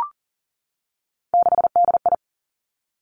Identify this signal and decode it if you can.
Morse code, heard as a keyed tone, about 39 wpm, 700 Hz. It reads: 6DI
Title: Three Challenging Letter-Number Combinations 40wpm